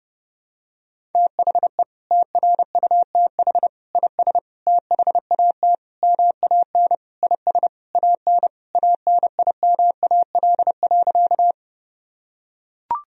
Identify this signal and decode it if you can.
Morse code, heard as a keyed tone, about 30 wpm, 700 Hz. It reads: THE TRUTH IS THAT MAN IS AN ANIMAL.
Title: Sentences from Top 1000 Words - Review Part 1 - 30wpm